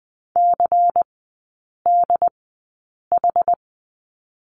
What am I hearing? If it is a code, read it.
Morse code, heard as a keyed tone, about 20 wpm, spaced as for 8 wpm, 690 Hz. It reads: CDH
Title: 166 Lesson 27e - Number 7 - Three Characters 20wpm